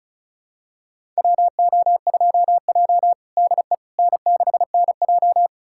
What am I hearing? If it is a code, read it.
Morse code, heard as a keyed tone, about 34 wpm, 700 Hz. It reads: WO2J DE N6NJ
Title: USA Callsign DE Callsign - 4 Characters - 35wpm